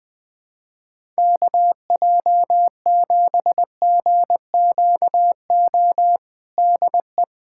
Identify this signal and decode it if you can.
Morse code, heard as a keyed tone, about 21 wpm, 700 Hz. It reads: KJ7GQO DE
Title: USA Callsign DE Callsign - 6 Characters - 20wpm